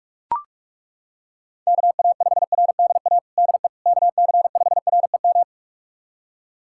Morse code, heard as a keyed tone, about 42 wpm, 700 Hz. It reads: KA5RDA DE KC5REM
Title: USA Callsign DE Callsign 45wpm